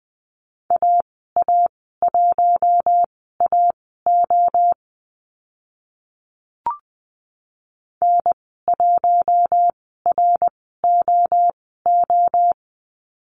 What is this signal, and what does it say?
Morse code, heard as a keyed tone, about 20 wpm, spaced as for 14 wpm, 700 Hz. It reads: AA1AO N1ROO
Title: 055 Lesson 10g - Letter r - Call Signs 20wpm